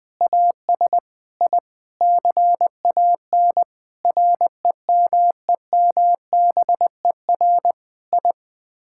Morse code, heard as a keyed tone, about 21 wpm, 700 Hz. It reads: AS I CAN REMEMBER I
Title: Sets of 8 Words - 20wpm